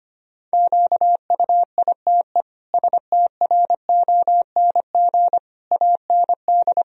Morse code, heard as a keyed tone, about 25 wpm, 700 Hz. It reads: QUITE STRONG AND
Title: Sentences from Top 1000 Words - Review Part 3 - 25wpm